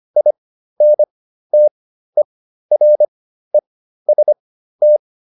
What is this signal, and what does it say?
Morse code, heard as a keyed tone, about 25 wpm, spaced as for 13 wpm, 600 Hz. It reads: INTEREST